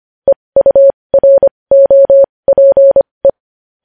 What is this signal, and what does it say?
Morse code, heard as a keyed tone, about 25 wpm, spaced as for 20 wpm, 550 Hz. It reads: EUROPE
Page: News Headlines in Morse Code at 20 WPM